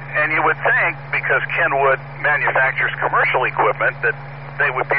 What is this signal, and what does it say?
Emergency Alert System (SAME) tones, heard as harsh buzzing bursts, with speech. It is heard from 1.1 to 4.2 s.